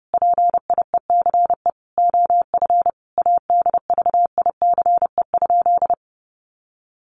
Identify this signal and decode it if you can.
Morse code, heard as a keyed tone, about 30 wpm, 700 Hz. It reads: PIECE OF ADVICE?
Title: Sentences from Top 1100 Words 30wpm